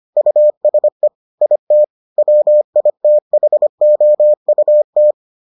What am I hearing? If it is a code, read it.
Morse code, heard as a keyed tone, about 25 wpm, 600 Hz. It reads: USE IT WITHOUT